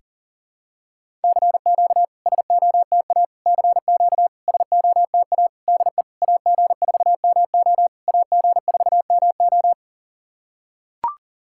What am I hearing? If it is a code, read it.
Morse code, heard as a keyed tone, about 39 wpm, 700 Hz. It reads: CQ SOTA CQ SOTA DE AG4MO AG4MO
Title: SOTA QSO Practice 40wpm